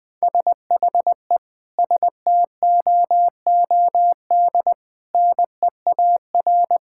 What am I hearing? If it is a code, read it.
Morse code, heard as a keyed tone, about 20 wpm, 700 Hz. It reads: SHE STOOD NEAR
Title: Sentences from Top 1600 Words 20wpm